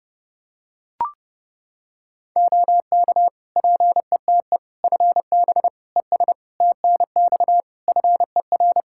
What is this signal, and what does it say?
Morse code, heard as a keyed tone, about 30 wpm, 700 Hz. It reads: OK PETE FB ES TNX FER
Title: Practice QSO - Exchange 2 - Info - 30wpm